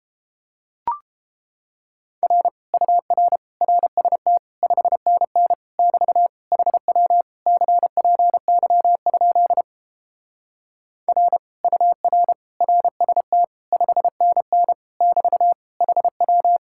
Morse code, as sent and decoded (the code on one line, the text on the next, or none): .-. / ..- .-. / .-. ... - / ..... -. -. / -...- / .... .-- / -.-. .--. -.-- ..--.. / .-. / ..- .-. / .-. ... - / ..... -. -. / -...- / .... .--
R UR RST 5NN <BT> HW CPY? R UR RST 5NN <BT> HW